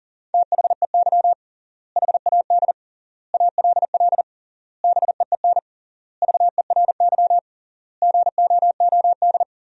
Morse code, heard as a keyed tone, about 40 wpm, 700 Hz. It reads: THEY HAD ALL BEEN VERY GOOD
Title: 3x Word-Spacing - Sentences from Top 200 Words - 40wpm